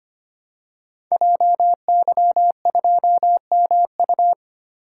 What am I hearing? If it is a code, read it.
Morse code, heard as a keyed tone, about 25 wpm, 700 Hz. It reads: JY2MU
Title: Code Groups - 5 Letter:Numbers - 25wpm